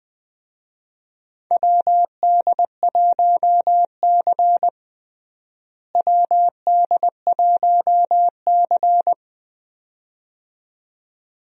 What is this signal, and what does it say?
Morse code, heard as a keyed tone, about 20 wpm, 700 Hz. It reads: WD1C WD1C
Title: USA Call Signs - 4 Characters - Repeated 2x - 20wpm